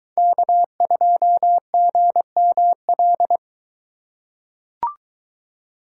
Morse code, heard as a keyed tone, about 22 wpm, 700 Hz. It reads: K2GML